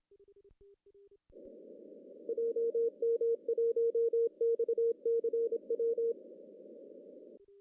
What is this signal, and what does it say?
Morse code, heard as a keyed tone, about 26 wpm, 460 Hz. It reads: JM1XCW
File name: CW / cqww-cw-k5tr